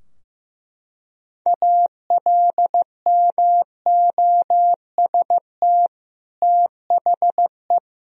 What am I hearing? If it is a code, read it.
Morse code, heard as a keyed tone, about 15 wpm, 700 Hz. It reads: ALMOST THE
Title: Sentences from Top 1000 Words - Review Part 4 - 15wpm